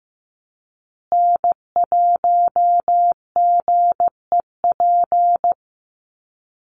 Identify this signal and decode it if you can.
Morse code, heard as a keyed tone, about 15 wpm, 700 Hz. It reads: N1GEP
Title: USA Call Signs Repeated 3x 15wpm